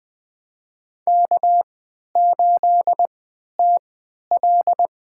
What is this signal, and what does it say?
Morse code, heard as a keyed tone, about 20 wpm, spaced as for 11 wpm, 700 Hz. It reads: K8TL
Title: USA Call Signs - 3x Character Spacing - 20wpm